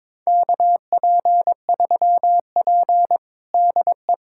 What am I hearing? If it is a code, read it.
Morse code, heard as a keyed tone, about 22 wpm, 700 Hz. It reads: KP3P DE
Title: USA Callsign DE Callsign - 4 Characters - 22wpm